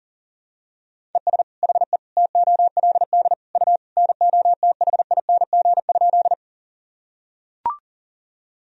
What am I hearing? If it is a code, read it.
Morse code, heard as a keyed tone, about 37 wpm, 700 Hz. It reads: ES HE TOLD U NOTHING?
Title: Sentences from Top 500 Words and Abbreviations 40wpm